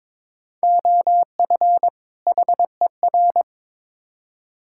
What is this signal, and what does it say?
Morse code, heard as a keyed tone, about 22 wpm, 700 Hz. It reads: OF HER
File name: Build A Phrase - 2-Word Phrases - 22wpm